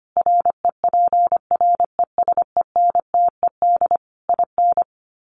Morse code, heard as a keyed tone, about 25 wpm, 700 Hz. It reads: REPRESENTED IN
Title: Sets of 5 Words 25wpm